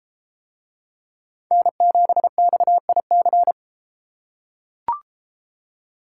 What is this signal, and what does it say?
Morse code, heard as a keyed tone, about 30 wpm, 700 Hz. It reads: N7XIC